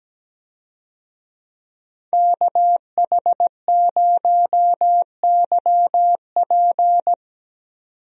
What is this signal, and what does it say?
Morse code, heard as a keyed tone, about 17 wpm, 700 Hz. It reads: KH0YP
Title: USA Call Signs - 5 Characters - 17wpm